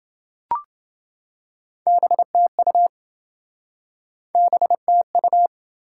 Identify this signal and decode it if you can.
Morse code, heard as a keyed tone, about 27 wpm, 700 Hz. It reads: BTU BTU